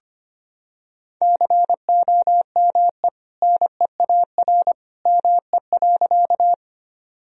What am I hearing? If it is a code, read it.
Morse code, heard as a keyed tone, about 25 wpm, 700 Hz. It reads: COME NEAR ME.
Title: Sentences from Top 1000 Words - Review Part 1 - 25wpm